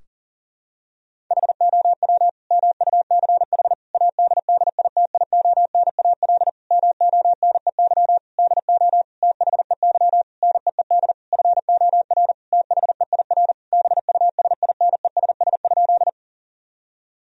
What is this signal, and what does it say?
Morse code, heard as a keyed tone, about 40 wpm, 700 Hz. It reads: HOW MUCH ADDITIONAL MONEY DO THEY NEED FOR THEIR BUSINESS?
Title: Sentences from Top 1000 Words - Review Part 4 - 40wpm